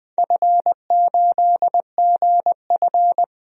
Morse code, heard as a keyed tone, about 20 wpm, 700 Hz. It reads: F8GF
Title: DXCC-100 Call Signs - 4 Characters - 20wpm